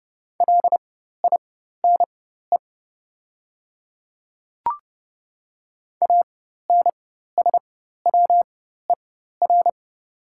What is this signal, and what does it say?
Morse code, heard as a keyed tone, about 30 wpm, spaced as for 14 wpm, 700 Hz. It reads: LINE ANSWER
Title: 4x Character Spacing - Top 300 Words - 30wpm